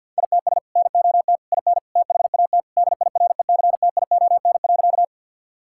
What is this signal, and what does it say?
Morse code, heard as a keyed tone, about 52 wpm, 700 Hz. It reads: ITS NOT IN THAT DIRECTION.
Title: Sentences from Top 1000 Words - Review Part 5 - 50wpm